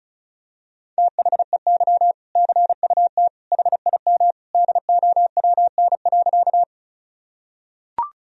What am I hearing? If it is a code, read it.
Morse code, heard as a keyed tone, about 36 wpm, 700 Hz. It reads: THEY CUT HIM DOWN.
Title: Sentences from Top 1000 Words - Review Part 1 - 35wpm